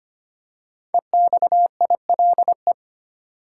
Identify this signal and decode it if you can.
Morse code, heard as a keyed tone, about 25 wpm, 700 Hz. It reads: EXILE
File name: 5 Letter Words - First 3 Letters Given - 25wpm